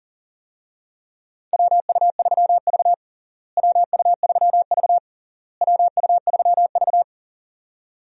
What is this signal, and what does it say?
Morse code, heard as a keyed tone, about 40 wpm, 700 Hz. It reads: WU3V WU3V WU3V
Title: USA Call Signs - 4 Characters - Repeated 3x - 40wpm